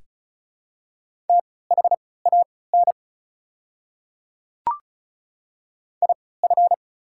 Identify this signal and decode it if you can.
Morse code, heard as a keyed tone, about 33 wpm, spaced as for 19 wpm, 690 Hz. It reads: THAN IF
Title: 3x Character Spacing - Top 100 Words - 35wpm